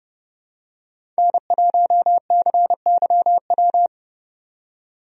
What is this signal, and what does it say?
Morse code, heard as a keyed tone, about 30 wpm, 700 Hz. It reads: N1CYW
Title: USA Call Signs - 5 Characters - 30wpm